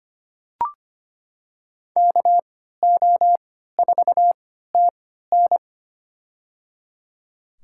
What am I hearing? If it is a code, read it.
Morse code, heard as a keyed tone, about 25 wpm, spaced as for 14 wpm, 700 Hz. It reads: KO4TN
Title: USA Call Signs - 5 Characters - 3x Character Spacing - 25wpm